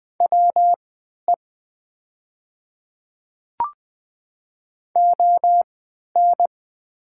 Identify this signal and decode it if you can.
Morse code, heard as a keyed tone, about 20 wpm, spaced as for 11 wpm, 700 Hz. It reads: WE ON
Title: 3x Character Spacing - Top 100 Words - 20wpm